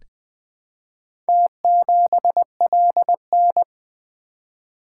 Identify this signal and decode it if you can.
Morse code, heard as a keyed tone, about 20 wpm, 700 Hz. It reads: T7LN
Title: Four Letter-Number Combinations - 20wpm